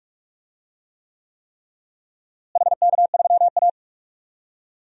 Morse code, heard as a keyed tone, about 46 wpm, 700 Hz. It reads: HK3A
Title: DXCC-100 Call Signs - 4 Characters - Repeated 3x - 45wpm